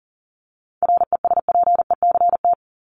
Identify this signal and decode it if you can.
Morse code, heard as a keyed tone, about 40 wpm, 700 Hz. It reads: RESPECT
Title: Top 1501-1750 Words 40wpm